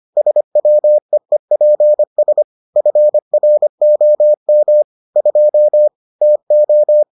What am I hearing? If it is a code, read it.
Morse code, heard as a keyed tone, about 25 wpm, 600 Hz. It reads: SWEEPS FROM 2 TO